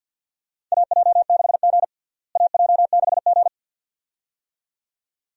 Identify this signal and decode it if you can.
Morse code, heard as a keyed tone, about 50 wpm, 700 Hz. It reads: AJ6G AJ6G
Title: USA Call Signs - 4 Characters - Repeated 2x - 50wpm